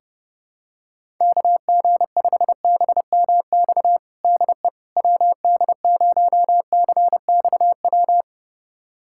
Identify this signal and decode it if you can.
Morse code, heard as a keyed tone, about 30 wpm, 700 Hz. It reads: KG5BMX DE WD0CXW
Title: USA Callsign DE Callsign - 6 Characters - 30wpm